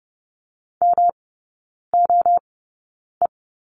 Morse code, heard as a keyed tone, about 30 wpm, spaced as for 9 wpm, 700 Hz. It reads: MOE
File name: Lesson 18e - Letter m - Three Characters 30wpm